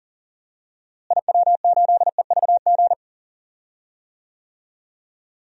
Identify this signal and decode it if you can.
Morse code, heard as a keyed tone, about 37 wpm, 700 Hz. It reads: IW8EVG
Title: DXCC-100 Call Signs - 6 Characters - Repeated 2x - 40wpm